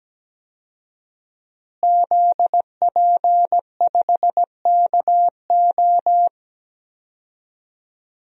Morse code, heard as a keyed tone, about 17 wpm, 700 Hz. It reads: ZP5KO
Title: DXCC-100 Call Signs - 5 Characters - Repeated 3x - 17wpm